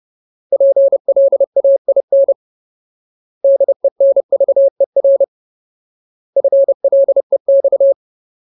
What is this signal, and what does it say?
Morse code, heard as a keyed tone, about 30 wpm, 550 Hz. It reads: PLAIN DENVER FLEX
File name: Ultimate Word List / 30WPM Ultimate